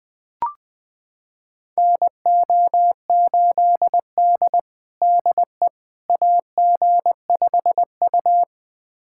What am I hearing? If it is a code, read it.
Morse code, heard as a keyed tone, about 21 wpm, 700 Hz. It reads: NO8D DE AG5U
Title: USA Callsign DE Callsign - 4 Characters - 20wpm